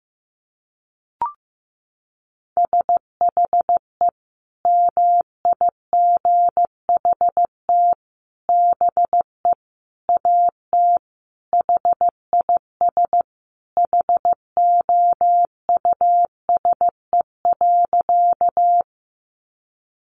Morse code, heard as a keyed tone, about 15 wpm, 700 Hz. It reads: SHE MIGHT BE AT HIS HOUSE.
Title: Sentences from Top 300 Words 15wpm